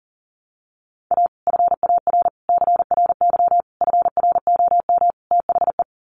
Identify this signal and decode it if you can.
Morse code, heard as a keyed tone, about 40 wpm, 700 Hz. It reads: A FAR CRY FROM THE
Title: Sets of 5 Words Encore 40wpm